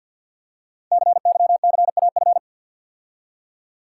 Morse code, heard as a keyed tone, about 48 wpm, 700 Hz. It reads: CYCAF
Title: Code Groups - 5 Letters - 50wpm